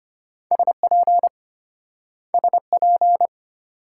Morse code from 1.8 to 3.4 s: SP

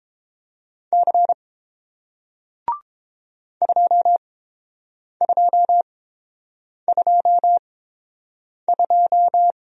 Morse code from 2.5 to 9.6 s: 2222

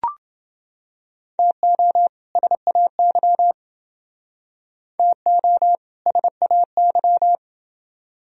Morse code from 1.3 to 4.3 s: TO SAY

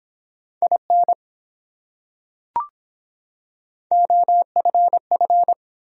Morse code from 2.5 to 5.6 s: OFF

